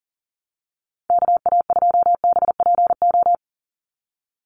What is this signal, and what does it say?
Morse code, heard as a keyed tone, about 40 wpm, 700 Hz. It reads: KA2BPO
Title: USA Call Signs 40wpm